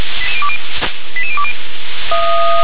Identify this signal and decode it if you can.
Five-tone selective call (ZVEI), heard as quick runs of five tones, tones of about 70 ms. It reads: 88020
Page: /357/SDR/ZVEI/88778/